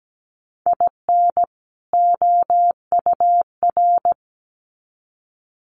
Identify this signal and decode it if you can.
Morse code, heard as a keyed tone, about 17 wpm, 700 Hz. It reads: IN OUR
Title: Sets of 2 Words 17wpm